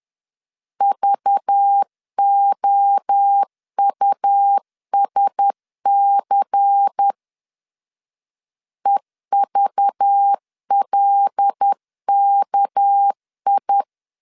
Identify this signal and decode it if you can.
Morse code, heard as a keyed tone, about 10 wpm, 800 Hz. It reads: VOUSC EVLKI
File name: LEZIONI / 7 GF9Z / 7 GF9Z 10wpm 2611